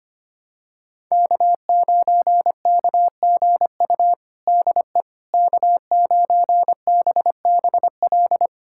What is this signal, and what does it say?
Morse code, heard as a keyed tone, about 25 wpm, 700 Hz. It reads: K9KGU DE K9BBL
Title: USA Callsign DE Callsign - 5 Characters - 25wpm